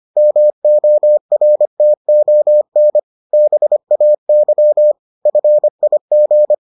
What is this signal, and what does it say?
Morse code, heard as a keyed tone, about 25 wpm, 600 Hz. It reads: MORTON BAY FIG